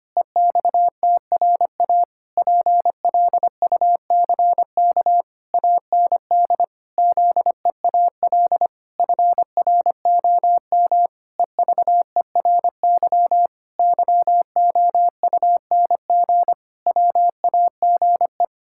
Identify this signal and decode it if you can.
Morse code, heard as a keyed tone, about 25 wpm, 700 Hz. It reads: EXTRA PLUCK AND ZEAL FROM EVERY YOUNG WAGE
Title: Pangram Challenge v2 - 25wpm